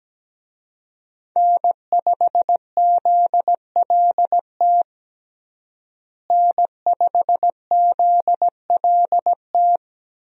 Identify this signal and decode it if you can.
Morse code, heard as a keyed tone, about 17 wpm, 700 Hz. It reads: N5ZLT N5ZLT
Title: USA Call Signs - 5 Characters - Repeated 2x - 17wpm